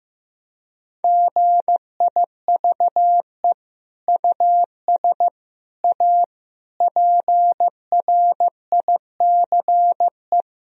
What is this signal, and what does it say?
Morse code, heard as a keyed tone, about 16 wpm, 700 Hz. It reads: GIVE US A PRICE
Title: Sentences from Top 1000 Words - Review Part 4 - 15wpm